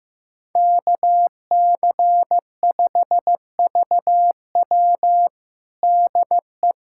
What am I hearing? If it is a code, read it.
Morse code, heard as a keyed tone, about 16 wpm, 700 Hz. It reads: KC5VW DE
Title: USA Callsign DE Callsign 15wpm